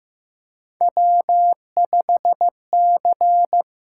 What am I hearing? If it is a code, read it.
Morse code, heard as a keyed tone, about 15 wpm, 700 Hz. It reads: W5C
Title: USA Call Signs - 4 Characters - 8x Character Spacing - 15wpm